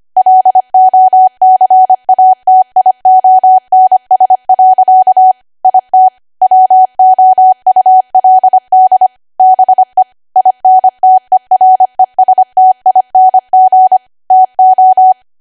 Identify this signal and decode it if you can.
Morse code, heard as a keyed tone, about 25 wpm, 750 Hz. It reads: LOCATIONS. IT WOULD BE INTERESTING TO
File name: Archive / 25 WPM / 25WPM